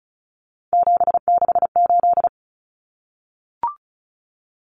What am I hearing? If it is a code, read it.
Morse code, heard as a keyed tone, about 34 wpm, 700 Hz. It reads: 768
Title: Common Cut Numbers - 3-Digit 35wpm